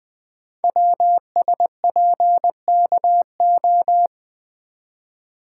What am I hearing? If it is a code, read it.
Morse code, heard as a keyed tone, about 20 wpm, 700 Hz. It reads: WSPKO
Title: Code Groups - 5 Letters - 20wpm